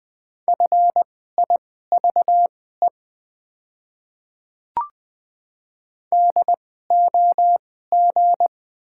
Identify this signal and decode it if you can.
Morse code, heard as a keyed tone, about 20 wpm, spaced as for 15 wpm, 690 Hz. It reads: FIVE DOG
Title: 2x Character Spacing - Top 500 Words - 20wpm